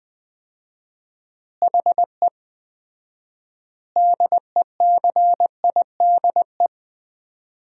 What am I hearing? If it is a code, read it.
Morse code, heard as a keyed tone, about 20 wpm, 700 Hz. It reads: HE DECIDE
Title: 4x Word-Spacing - Sentences from Top 500 Words - 20wpm